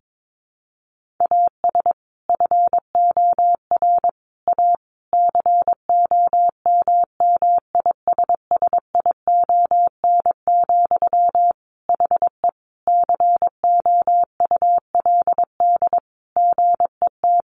Morse code, heard as a keyed tone, about 22 wpm, 700 Hz. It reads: AS FOR A COMMISSION, HE COULD GET
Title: Sentences from Top 1400 Words 22wpm